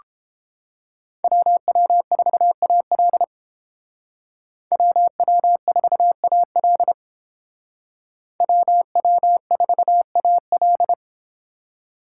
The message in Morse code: WW4AL WW4AL WW4AL

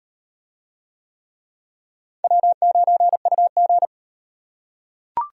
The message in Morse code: W9UG

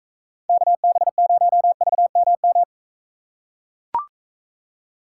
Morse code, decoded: KB0VMM

